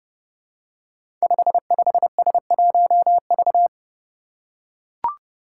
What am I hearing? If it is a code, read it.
Morse code, heard as a keyed tone, about 30 wpm, 700 Hz. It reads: 55S1V
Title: Five Challenging Letter-Number Combinations 30wpm